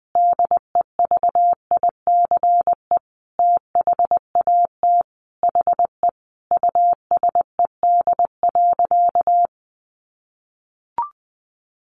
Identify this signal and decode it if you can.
Morse code, heard as a keyed tone, about 20 wpm, 700 Hz. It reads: DEVICE THAT HE USED.
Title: Sentences from Top 1500 Words 20wpm